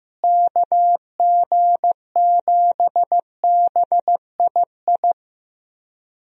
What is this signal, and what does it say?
Morse code, heard as a keyed tone, about 15 wpm, 700 Hz. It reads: KG7BII
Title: USA Call Signs - 6 Characters - Repeated 2x - 15wpm